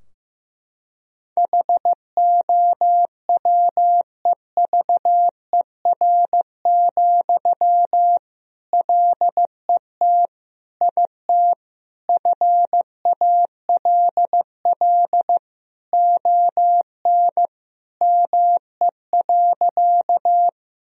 Morse code, heard as a keyed tone, about 15 wpm, 700 Hz. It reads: HOWEVER, LET IT FALL ON ME.
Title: Sentences from Top 1000 Words - Review Part 2 - 15wpm